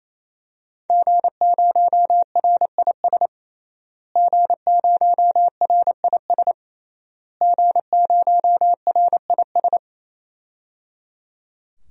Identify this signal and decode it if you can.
Morse code, heard as a keyed tone, about 28 wpm, 700 Hz. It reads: G0RIS G0RIS G0RIS
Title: DXCC-100 Call Signs - 5 Characters - Repeated 3x - 28wpm